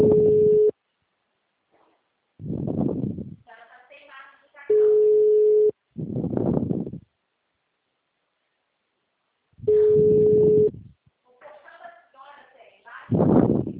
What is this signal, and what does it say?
Ringback tone (Europe, 425 Hz), with speech.